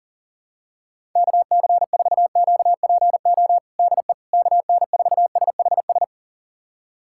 Morse code, heard as a keyed tone, about 40 wpm, 700 Hz. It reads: KC4QPO DE KN4SHS
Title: USA Callsign DE Callsign 40wpm